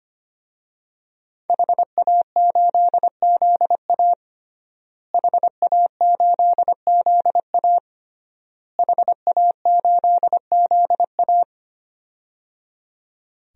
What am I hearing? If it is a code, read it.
Morse code, heard as a keyed tone, about 25 wpm, 700 Hz. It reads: HA8ZA HA8ZA HA8ZA
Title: DXCC-100 Call Signs - 5 Characters - Repeated 3x - 25wpm